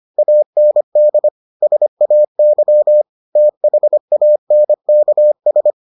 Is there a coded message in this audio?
Yes: Morse code from 0.2 to 5.7 s, heard as a keyed tone, about 25 wpm, 600 Hz. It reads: AND SAY THANKS